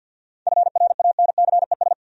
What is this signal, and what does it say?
Morse code, heard as a keyed tone, about 47 wpm, 700 Hz. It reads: FRANCES
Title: USA Common Names - 50wpm